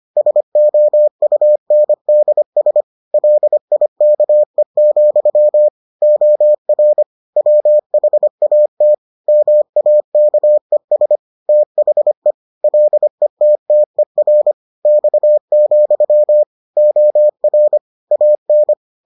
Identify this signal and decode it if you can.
Morse code, heard as a keyed tone, about 25 wpm, 600 Hz. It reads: SOUNDS LIKE, OR WHAT MAKES THE LETTER X, OR AN